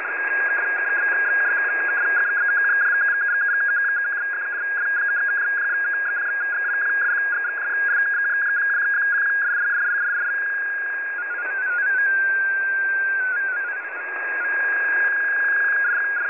Two-tone teleprinter signal, about 200 Hz shift.
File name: file21042 / websdr recording start t15 39 15z 10710.5khz